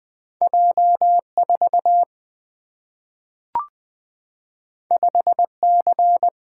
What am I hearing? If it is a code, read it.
Morse code, heard as a keyed tone, about 20 wpm, 700 Hz. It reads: J 4 5 C
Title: Single-Letter Single-Number 20wpm